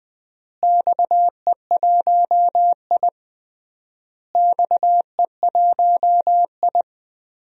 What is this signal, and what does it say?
Morse code, heard as a keyed tone, about 20 wpm, 700 Hz. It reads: XE1I XE1I
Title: DXCC-100 Call Signs - 4 Characters - Repeated 3x - 20wpm